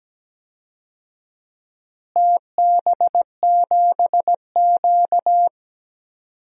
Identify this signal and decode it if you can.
Morse code, heard as a keyed tone, about 17 wpm, 700 Hz. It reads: TB7Q
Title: DXCC-100 Call Signs - 4 Characters - Repeated 2x - 17wpm